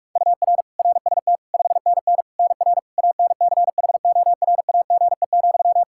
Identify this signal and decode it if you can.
Morse code, heard as a keyed tone, about 46 wpm, 700 Hz. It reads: UR RST 5NN NR ANCHORAGE,
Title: Practice QSO - Exchange 1 - Beginning - 45wpm